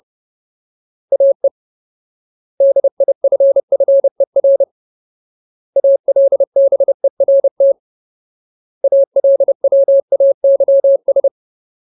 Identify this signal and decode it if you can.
Morse code, heard as a keyed tone, about 30 wpm, 550 Hz. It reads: AE DIFFER ALBERT ALWAYS
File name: Ultimate Word List / 30WPM Ultimate